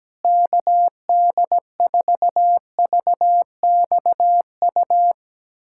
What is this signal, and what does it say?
Morse code, heard as a keyed tone, about 17 wpm, 700 Hz. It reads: KD4VXU
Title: USA Callsign DE Callsign - 6 Characters - 17wpm